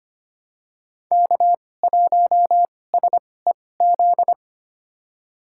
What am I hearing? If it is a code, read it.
Morse code, heard as a keyed tone, about 25 wpm, spaced as for 18 wpm, 700 Hz. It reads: K1SEZ
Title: USA Call Signs - 2x Character Spacing - 25wpm